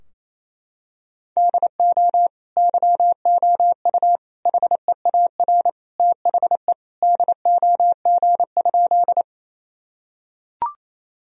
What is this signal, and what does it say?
Morse code, heard as a keyed tone, about 28 wpm, 700 Hz. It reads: DO YOU HEAR THE DOG?
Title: Sentences from Top 1000 Words - Review Part 2 - 28wpm